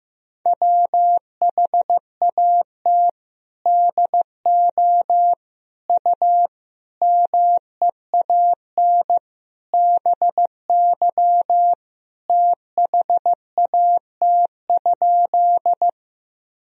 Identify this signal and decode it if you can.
Morse code, heard as a keyed tone, about 15 wpm, 700 Hz. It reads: WHAT DO U MEAN BY THAT?
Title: Sentences from Top 200 Words and Abbreviations 15wpm